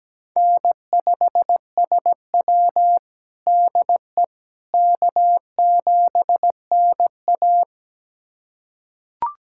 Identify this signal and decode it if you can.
Morse code, heard as a keyed tone, about 17 wpm, 700 Hz. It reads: N5SW DE K7NA
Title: USA Callsign DE Callsign - 4 Characters - 17wpm